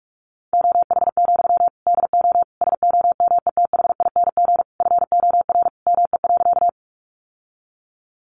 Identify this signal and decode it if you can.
Morse code, heard as a keyed tone, about 46 wpm, 700 Hz. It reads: OH, DO SOMETHING FOR ME.
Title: Sentences from Top 600 Words 45wpm